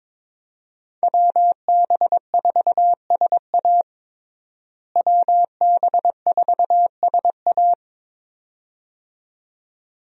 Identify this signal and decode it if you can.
Morse code, heard as a keyed tone, about 22 wpm, 700 Hz. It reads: WB4SA WB4SA
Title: USA Call Signs - 5 Characters - Repeated 2x - 22wpm